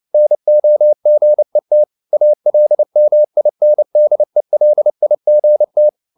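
Morse code, heard as a keyed tone, about 29 wpm, 600 Hz. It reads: NOGET ALMINDELIGT
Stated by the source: Fast (QRQ) Morse at 29 wpm, from CW med Gnister.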